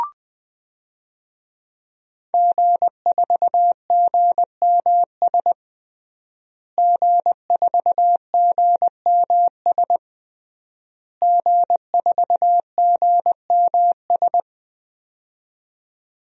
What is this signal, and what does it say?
Morse code, heard as a keyed tone, about 20 wpm, 700 Hz. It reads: G4GMS G4GMS G4GMS
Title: DXCC-100 Call Signs - 5 Characters - Repeated 3x - 20wpm